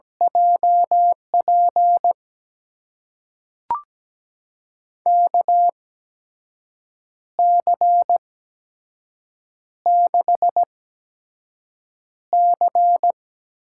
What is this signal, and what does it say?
Morse code, heard as a keyed tone, about 17 wpm, spaced as for 6 wpm, 700 Hz. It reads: JP KC6C
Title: USA Call Signs - 6 Characters - 8x Character Spacing - 17wpm